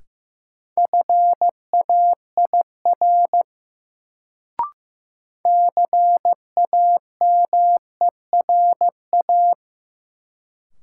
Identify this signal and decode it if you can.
Morse code, heard as a keyed tone, about 15 wpm, 700 Hz. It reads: FAIR CAMERA
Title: Sentences from Top 1000 Words - Review Part 5 - New Word Drill - 15wpm